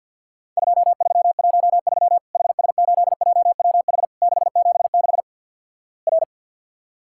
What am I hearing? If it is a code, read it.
Morse code, heard as a keyed tone, about 46 wpm, 700 Hz. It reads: 2313 HS8JWH 676 R
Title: Worked All Europe DX Contest - QTC Exchange - 50wpm